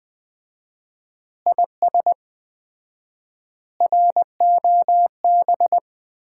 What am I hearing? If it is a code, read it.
Morse code, heard as a keyed tone, about 20 wpm, 700 Hz. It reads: IS ROB
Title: Common QSO Phrases - 4x Word-Spacing - 20wpm